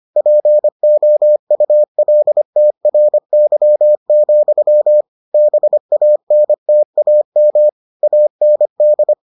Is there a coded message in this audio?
Yes: Morse code from 0.2 to 9.1 s, heard as a keyed tone, about 25 wpm, 600 Hz. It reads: POULTRY, BANTAM AND